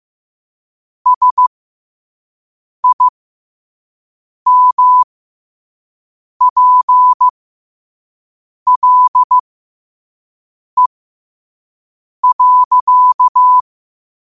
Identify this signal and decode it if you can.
Morse code, heard as a keyed tone, about 15 wpm, spaced as for 5 wpm, 1000 Hz. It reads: SIMPLE.
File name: qotdfiles / 5wpm